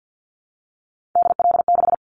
Morse code, heard as a keyed tone, about 50 wpm, 700 Hz. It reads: DL6
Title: Callsign Prefixes for Top 25 DXCC - 50wpm